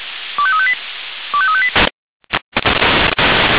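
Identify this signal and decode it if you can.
Five-tone selective call (ZVEI), heard as quick runs of five tones, tones of about 70 ms. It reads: 25358 25358
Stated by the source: ZVEI five-tone selective call received on SDR.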